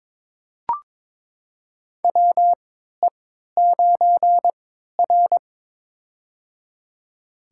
Morse code, heard as a keyed tone, about 22 wpm, spaced as for 13 wpm, 700 Hz. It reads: WE9R
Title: USA Call Signs - 4 Characters - 3x Character Spacing - 22wpm